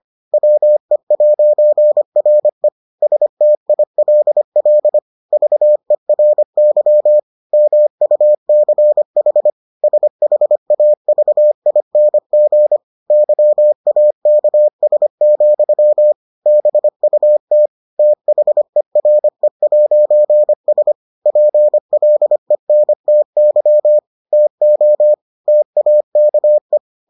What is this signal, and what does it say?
Morse code, heard as a keyed tone, about 25 wpm, 600 Hz. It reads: WE'RE STILL VERY MUCH SHAVING YAKS, BUT THERE'S PLENTY TO TAKE